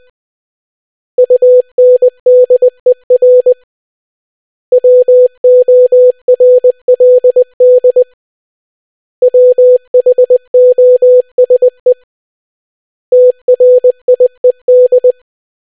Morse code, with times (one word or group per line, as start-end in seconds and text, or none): 1.2-3.5 UNDER
4.7-8.0 WORLD
9.2-11.9 WHOSE
13.1-15.1 TRIED